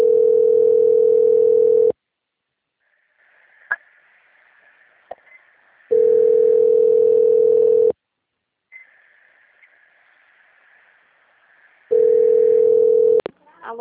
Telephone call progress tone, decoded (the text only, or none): ringback tone